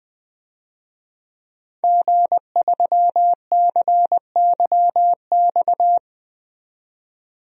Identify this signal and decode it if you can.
Morse code, heard as a keyed tone, about 20 wpm, 700 Hz. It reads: G3CYX
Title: DXCC-100 Call Signs - 5 Characters - 20wpm